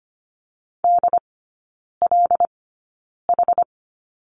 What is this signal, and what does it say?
Morse code, heard as a keyed tone, about 25 wpm, spaced as for 9 wpm, 700 Hz. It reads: DLH
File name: Lesson 17e - Letter u - Three Characters 25wpm